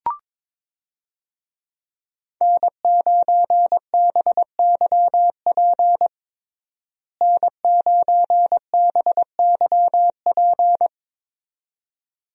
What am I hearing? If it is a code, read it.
Morse code, heard as a keyed tone, about 21 wpm, 700 Hz. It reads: N9BYP N9BYP
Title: USA Call Signs - 5 Characters - Repeated 2x - 22wpm